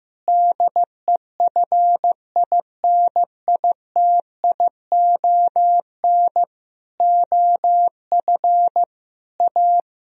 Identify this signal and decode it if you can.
Morse code, heard as a keyed tone, about 15 wpm, 700 Hz. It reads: DEFINITION OF A
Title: Sentences from Top 1000 Words - Review Part 4 - 15wpm